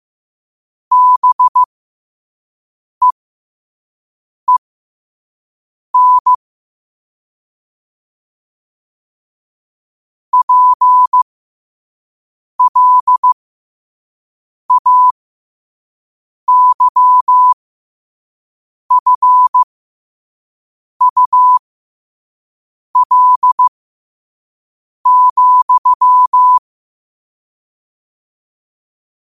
Morse code, heard as a keyed tone, about 15 wpm, spaced as for 5 wpm, 1000 Hz. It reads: BEEN PLAYFUL,
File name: qotdfiles / 5wpm